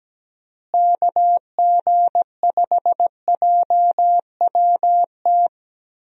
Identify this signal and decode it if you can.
Morse code, heard as a keyed tone, about 17 wpm, 700 Hz. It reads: KG5JWT
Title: USA Call Signs - 6 Characters - Repeated 3x - 17wpm